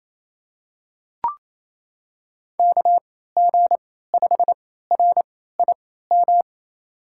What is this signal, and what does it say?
Morse code, heard as a keyed tone, about 28 wpm, spaced as for 16 wpm, 700 Hz. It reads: KG5RIM
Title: USA Call Signs - 6 Characters - 3x Character Spacing - 28wpm